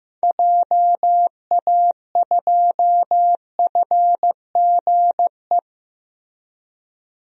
Morse code, heard as a keyed tone, about 15 wpm, 700 Hz. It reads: JA2FGE
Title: DXCC-100 Call Signs - 6 Characters - Repeated 2x - 15wpm